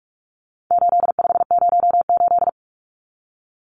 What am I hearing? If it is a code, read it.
Morse code, heard as a keyed tone, about 44 wpm, 700 Hz. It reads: 8508
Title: Numbers - 4-Digit 45wpm